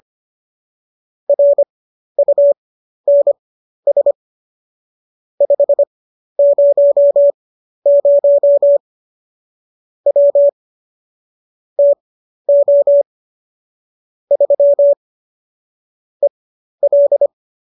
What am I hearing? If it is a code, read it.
Morse code, heard as a keyed tone, about 25 wpm, spaced as for 12 wpm, 570 Hz. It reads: RUNS 500 W TO 3 EL